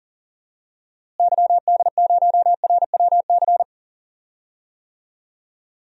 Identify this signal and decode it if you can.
Morse code, heard as a keyed tone, about 40 wpm, 700 Hz. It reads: YD0RWC
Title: DXCC-100 Call Signs - 6 Characters - Repeated 2x - 40wpm